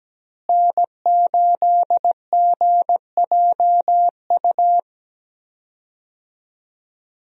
Morse code, heard as a keyed tone, about 17 wpm, 700 Hz. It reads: N8GJU
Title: USA Call Signs - 5 Characters - Repeated 2x - 17wpm